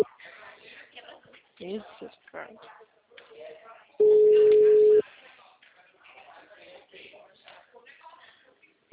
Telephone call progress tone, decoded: ringback tone

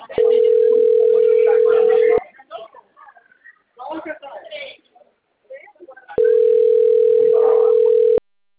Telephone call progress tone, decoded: ringback tone

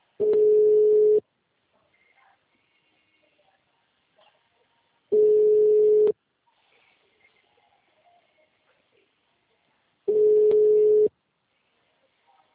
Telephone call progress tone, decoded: ringback tone

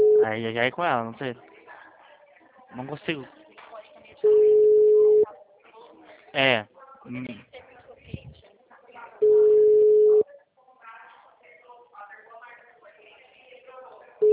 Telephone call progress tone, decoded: ringback tone

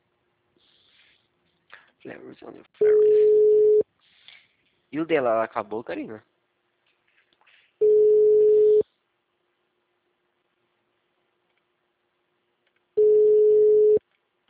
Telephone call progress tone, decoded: ringback tone

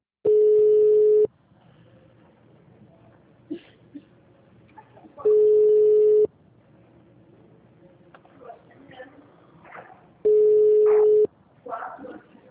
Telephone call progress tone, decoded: ringback tone

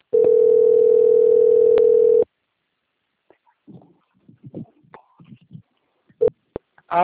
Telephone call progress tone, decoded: ringback tone

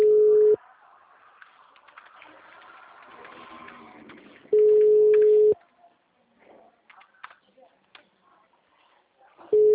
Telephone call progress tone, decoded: ringback tone

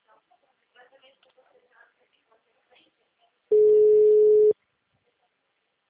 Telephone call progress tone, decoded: ringback tone

